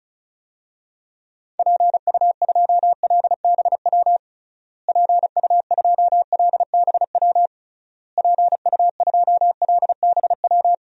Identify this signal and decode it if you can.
Morse code, heard as a keyed tone, about 36 wpm, 700 Hz. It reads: PU2LBW PU2LBW PU2LBW
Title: DXCC-100 Call Signs - 6 Characters - Repeated 3x - 35wpm